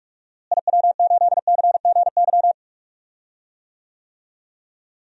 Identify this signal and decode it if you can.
Morse code, heard as a keyed tone, about 46 wpm, 700 Hz. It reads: IW8CGY
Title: DXCC-100 Call Signs - 6 Characters - Repeated 3x - 45wpm